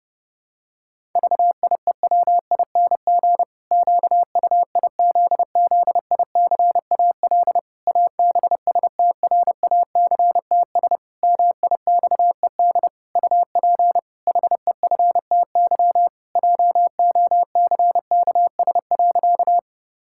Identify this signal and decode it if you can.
Morse code, heard as a keyed tone, about 32 wpm, 700 Hz. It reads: VIEWING QUIZZICAL ABSTRACTS MIXED UP HEFTY JOCKS.
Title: Pangram Challenge v2 - 30wpm